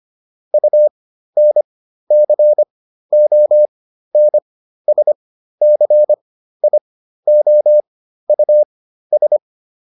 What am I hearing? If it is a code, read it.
Morse code, heard as a keyed tone, about 25 wpm, spaced as for 13 wpm, 600 Hz. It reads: UNCONSCIOUS